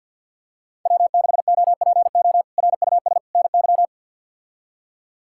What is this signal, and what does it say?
Morse code, heard as a keyed tone, about 53 wpm, 700 Hz. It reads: W6OPO RUS NY
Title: K1USN SST QSO Practice - 50wpm